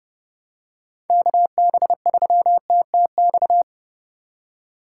Morse code, heard as a keyed tone, about 30 wpm, 700 Hz. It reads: KB3TTX
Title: USA Call Signs - 6 Characters - 3x Character Spacing - 30wpm